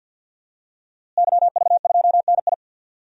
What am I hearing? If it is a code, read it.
Morse code, heard as a keyed tone, about 50 wpm, 700 Hz. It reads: YV2NI
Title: DXCC-100 Call Signs - 5 Characters - Repeated 2x - 50wpm